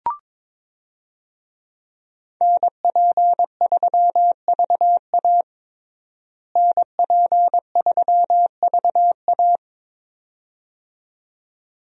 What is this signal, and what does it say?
Morse code, heard as a keyed tone, about 22 wpm, 700 Hz. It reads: NP3VA NP3VA
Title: USA Call Signs - 5 Characters - Repeated 2x - 22wpm